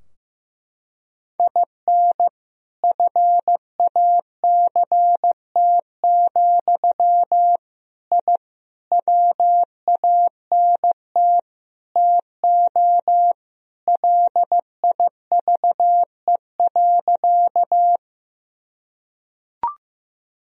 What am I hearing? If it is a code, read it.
Morse code, heard as a keyed tone, about 15 wpm, 700 Hz. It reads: IN FACT, I WANT TO LIVE.
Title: Sentences from Top 1000 Words - Review Part 3 - 15wpm